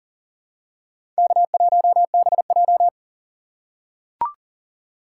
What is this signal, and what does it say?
Morse code, heard as a keyed tone, about 40 wpm, 700 Hz. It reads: K1BJ
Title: USA Call Signs - 4 Characters - 2x Character Spacing - 40wpm